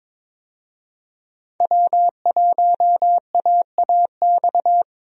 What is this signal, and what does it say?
Morse code, heard as a keyed tone, about 22 wpm, 700 Hz. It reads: W1AAX
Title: USA Call Signs - 5 Characters - Repeated 2x - 22wpm